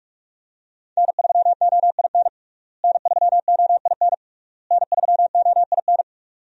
Morse code, heard as a keyed tone, about 42 wpm, 700 Hz. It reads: N3OIN N3OIN N3OIN
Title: USA Call Signs - 5 Characters - Repeated 3x - 45wpm